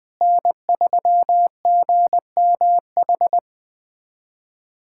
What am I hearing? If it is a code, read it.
Morse code, heard as a keyed tone, about 20 wpm, 700 Hz. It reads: N3GMH
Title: USA Callsign DE Callsign - 5 Characters - 20wpm